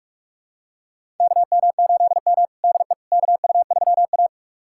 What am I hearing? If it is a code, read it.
Morse code, heard as a keyed tone, about 47 wpm, 700 Hz. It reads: KM8M DE KU3A
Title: USA Callsign DE Callsign - 4 Characters - 45wpm